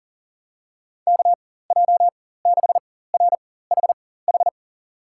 Morse code, heard as a keyed tone, about 40 wpm, spaced as for 19 wpm, 700 Hz. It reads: KJ6RHH
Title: USA Call Signs - 6 Characters - 4x Character Spacing - 40wpm